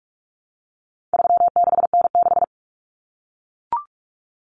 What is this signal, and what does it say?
Morse code, heard as a keyed tone, about 44 wpm, 700 Hz. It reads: 36N6
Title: Common Cut Numbers - 4-Digit 45wpm